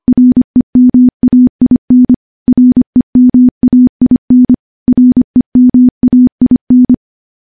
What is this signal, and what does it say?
Morse code, heard as a keyed tone, about 25 wpm, 250 Hz. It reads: REMAIN REMAIN REMAIN
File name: morse-training-25WPM